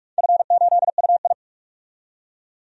Morse code, heard as a keyed tone, about 44 wpm, 700 Hz. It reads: F8UI
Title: DXCC-100 Call Signs - 4 Characters - 45wpm